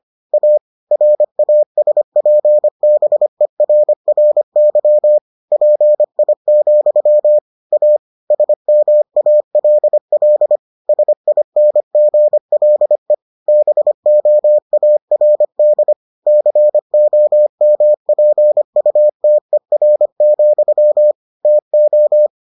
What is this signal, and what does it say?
Morse code, heard as a keyed tone, about 25 wpm, 600 Hz. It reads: A RASPBERRY PI, A SMALL SINGLE BOARD COMPUTER, TO